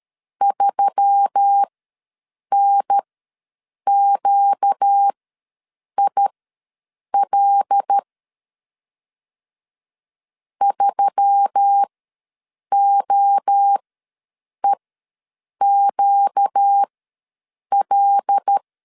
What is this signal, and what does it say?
Morse code, heard as a keyed tone, about 13 wpm, spaced as for 7 wpm, 800 Hz. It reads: 3NQIL 3OEQL
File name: LEZIONI / 5 LBVC / 5 L ws 12wpm 2958